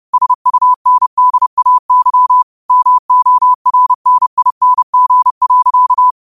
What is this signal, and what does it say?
Morse code, heard as a keyed tone, about 30 wpm, 1000 Hz. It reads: SUNDAY MORNING.
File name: qotdfiles / 30wpm